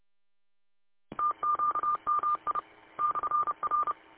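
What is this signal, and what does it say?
Morse code, heard as a keyed tone, about 29 wpm, 1200 Hz. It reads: TQMI /R
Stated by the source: Morse code recorded by W8UM on 145.230.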